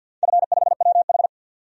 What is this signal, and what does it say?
Morse code, heard as a keyed tone, about 50 wpm, 700 Hz. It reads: F5WH
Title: DXCC-100 Call Signs - 4 Characters - 50wpm